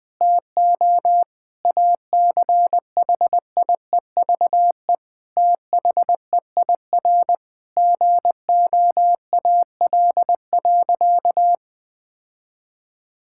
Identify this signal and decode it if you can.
Morse code, heard as a keyed tone, about 20 wpm, 700 Hz. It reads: TO ACHIEVE THEIR GOAL.
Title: Sentences from Top 1000 Words - Review Part 6 - 20wpm